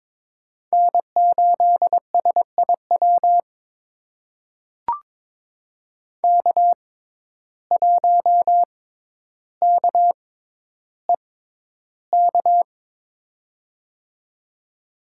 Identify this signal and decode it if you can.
Morse code, heard as a keyed tone, about 22 wpm, spaced as for 11 wpm, 700 Hz. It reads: N8SIW K1KEK
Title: USA Call Signs - 5 Characters - 6x Character Spacing - 22wpm